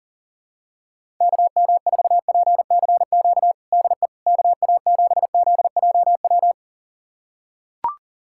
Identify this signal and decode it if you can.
Morse code, heard as a keyed tone, about 40 wpm, 700 Hz. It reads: KM4PCQ DE KA7ZJW
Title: USA Callsign DE Callsign - 6 Characters - 40wpm